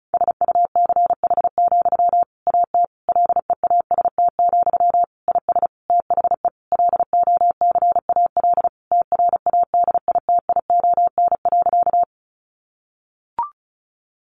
Morse code, heard as a keyed tone, about 34 wpm, 700 Hz. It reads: SUCH, AT LEAST, IS THE LOCAL TRADITION.
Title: Sentences from Top 1500 Words 35wpm